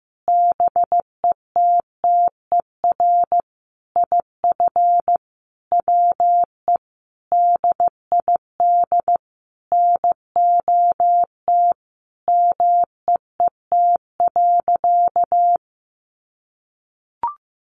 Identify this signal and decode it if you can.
Morse code, heard as a keyed tone, about 15 wpm, 700 Hz. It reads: BETTER IF WE DID NOT MEET.
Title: Sentences from Top 1100 Words 15wpm